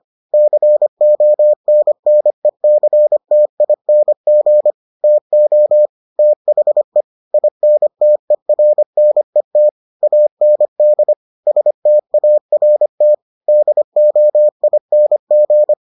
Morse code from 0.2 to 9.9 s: CONNECTING TO THE INTERNET

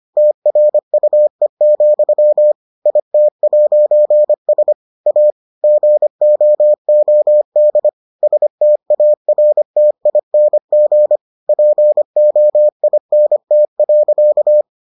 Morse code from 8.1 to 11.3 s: STARTING

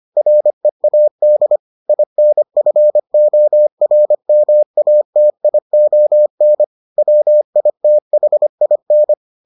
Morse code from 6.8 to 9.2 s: WITHIN

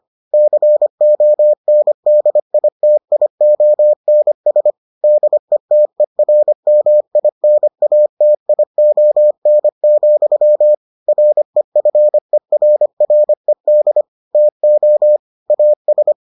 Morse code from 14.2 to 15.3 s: TO

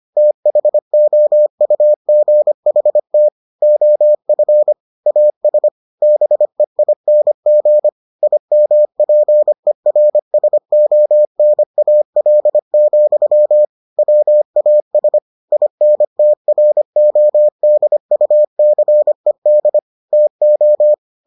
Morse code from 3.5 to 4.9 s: OF

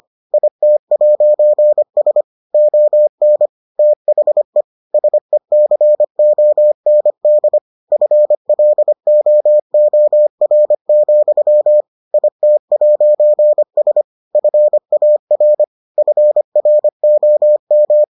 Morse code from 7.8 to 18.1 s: FLOOR, IT'S FAR FROM

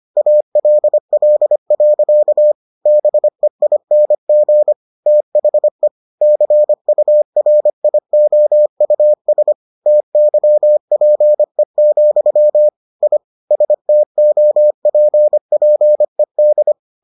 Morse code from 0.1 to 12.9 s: ALL. BEING THE CURIOUS TYPE,

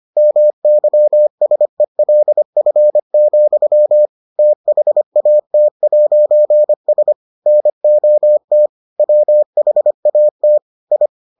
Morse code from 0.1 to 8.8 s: MYSELF, THAT'S NOT

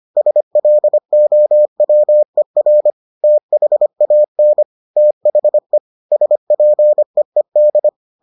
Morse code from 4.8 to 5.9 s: THE

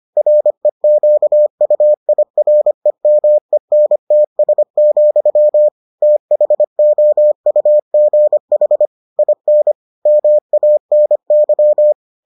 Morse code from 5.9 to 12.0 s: THOUGH IN MANY

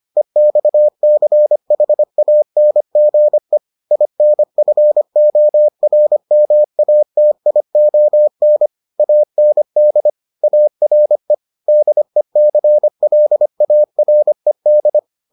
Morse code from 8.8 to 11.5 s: AND ARE

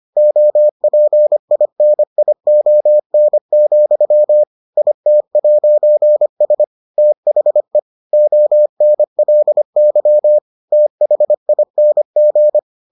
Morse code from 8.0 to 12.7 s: ONLY THING

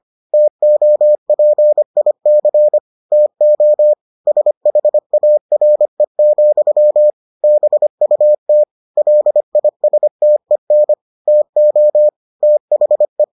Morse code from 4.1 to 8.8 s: SHARE, BUT

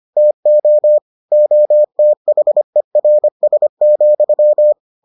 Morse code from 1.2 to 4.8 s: OTHERS,

